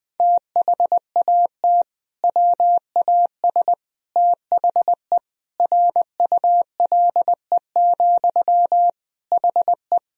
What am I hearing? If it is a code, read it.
Morse code, heard as a keyed tone, about 20 wpm, 700 Hz. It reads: THAT WAS THE RULE, HE
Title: Sentences from Top 1000 Words - Review Part 2 - 20wpm